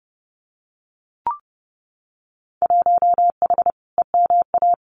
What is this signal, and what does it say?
Morse code, heard as a keyed tone, about 29 wpm, 700 Hz. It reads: 1H EMA
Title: Winter Field Day Exchanges 30wpm